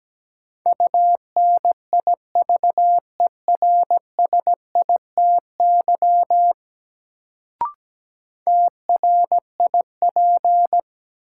Morse code, heard as a keyed tone, about 17 wpm, 700 Hz. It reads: UNIVERSITY TRIP
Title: Sentences from Top 1000 Words - Review Part 4 - New Word Drill - 17wpm